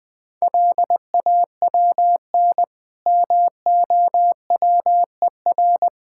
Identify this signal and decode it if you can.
Morse code, heard as a keyed tone, about 22 wpm, 700 Hz. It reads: LAWN MOWER
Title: Difficult Words - Speed-Racing with Hints - 20wpm